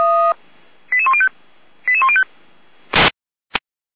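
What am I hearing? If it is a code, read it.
Five-tone selective call (ZVEI), heard as quick runs of five tones, tones of about 70 ms. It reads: 88105 88105
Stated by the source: Received by SDR; ZVEI five-tone call 88119, channel 455.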